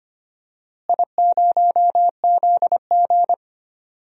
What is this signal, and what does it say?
Morse code, heard as a keyed tone, about 25 wpm, 700 Hz. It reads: I0ZG
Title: DXCC-100 Call Signs - 4 Characters - Repeated 3x - 25wpm